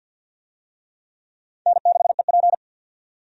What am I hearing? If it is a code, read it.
Morse code, heard as a keyed tone, about 53 wpm, 700 Hz. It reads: N6EP
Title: USA Call Signs - 4 Characters - 50wpm